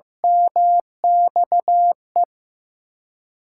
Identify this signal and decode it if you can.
Morse code, heard as a keyed tone, about 15 wpm, 700 Hz. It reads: MXE
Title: USA Call Signs - 6x Character Spacing - 15wpm